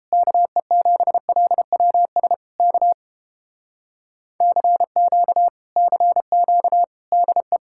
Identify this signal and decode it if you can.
Morse code, heard as a keyed tone, about 29 wpm, 700 Hz. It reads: KE7LWS K CQ CQ DE